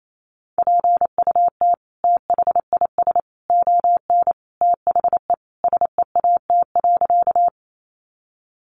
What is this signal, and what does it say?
Morse code, heard as a keyed tone, about 28 wpm, 700 Hz. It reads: PUT THIS ON THE SEAT.
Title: Sentences from Top 1500 Words 28wpm